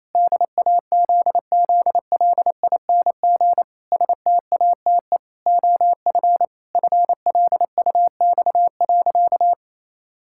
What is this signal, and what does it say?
Morse code, heard as a keyed tone, about 27 wpm, 700 Hz. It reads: DAZZLING STATE OF FLUX.
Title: Pangram Challenge v2 - 28wpm